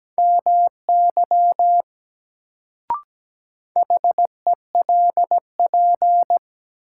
Morse code, heard as a keyed tone, about 17 wpm, 700 Hz. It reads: MY HELP
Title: Sets of 2 Words - New Word Drill - 17wpm